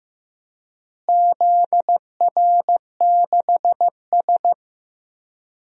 Morse code, heard as a keyed tone, about 15 wpm, 700 Hz. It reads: ZR6S
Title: DXCC-100 Call Signs - 4 Characters - Repeated 3x - 15wpm